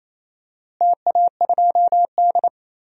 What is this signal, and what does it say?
Morse code, heard as a keyed tone, about 27 wpm, 700 Hz. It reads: TA2D
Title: DXCC-100 Call Signs - 4 Characters - Repeated 3x - 28wpm